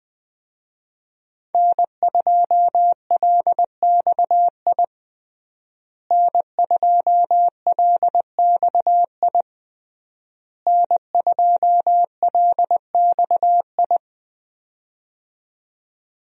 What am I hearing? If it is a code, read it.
Morse code, heard as a keyed tone, about 21 wpm, 700 Hz. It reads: N2LXI N2LXI N2LXI
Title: USA Call Signs - 5 Characters - Repeated 3x - 20wpm